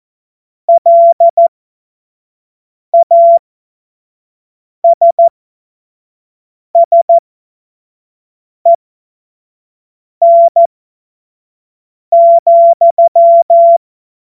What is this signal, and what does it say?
Morse code, heard as a keyed tone, about 14 wpm, spaced as for 5 wpm, 680 Hz. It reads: LASSEN,
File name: Morsekurs / Witze / Witz 04. Januar 2022, 25 F70